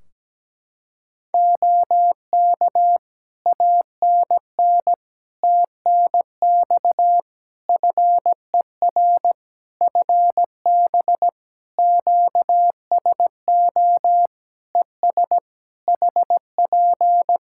Morse code, heard as a keyed tone, about 17 wpm, 700 Hz. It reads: OK ANN TNX FER FB QSO ES HP
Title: Practice QSO - Exchange 4 - Ending - 17wpm